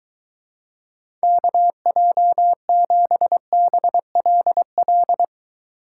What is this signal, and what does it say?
Morse code, heard as a keyed tone, about 23 wpm, 700 Hz. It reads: KJ7BLL